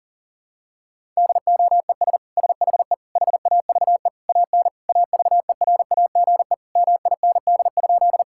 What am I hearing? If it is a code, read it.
Morse code, heard as a keyed tone, about 40 wpm, 700 Hz. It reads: DOES SHE HAVE AN AVERAGE MIND?
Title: Sentences from Top 1000 Words - Review Part 6 - 40wpm